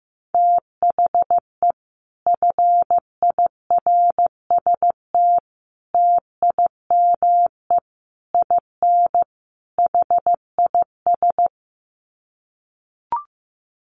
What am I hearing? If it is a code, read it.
Morse code, heard as a keyed tone, about 15 wpm, 700 Hz. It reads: THE FIRST TIME IN HIS
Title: Sets of 5 Words Encore2 15wpm